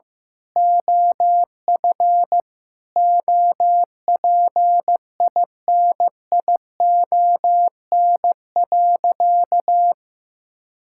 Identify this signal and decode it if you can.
Morse code, heard as a keyed tone, about 15 wpm, 700 Hz. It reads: OF OPINION.
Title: Sentences from Top 1600 Words 15wpm